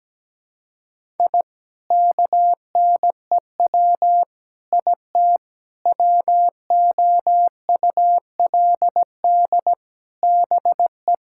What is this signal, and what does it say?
Morse code, heard as a keyed tone, about 17 wpm, 700 Hz. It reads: I KNEW IT WOULD BE
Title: Sentences from Top 1000 Words - Review Part 6 - 17wpm